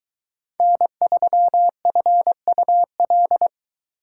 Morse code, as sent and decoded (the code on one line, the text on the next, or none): -. ...-- ..-. ..- .-..
N3FUL